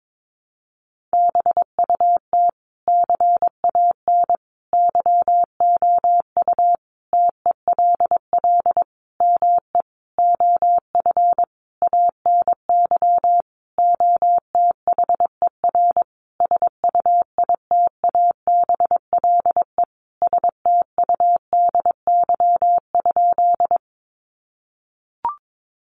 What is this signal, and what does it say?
Morse code, heard as a keyed tone, about 22 wpm, 700 Hz. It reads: BUT CAN YOU TELL ME OF ANY OTHER SUITABLE STUDY?
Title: Sentences from Top 1400 Words 22wpm